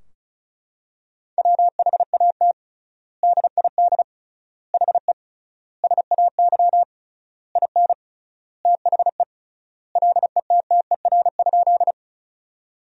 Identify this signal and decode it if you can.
Morse code, heard as a keyed tone, about 36 wpm, 700 Hz. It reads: WHAT DID HE SAY IN THE LETTER?
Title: 3x Word-Spacing - Sentences from Top 300 Words - 35wpm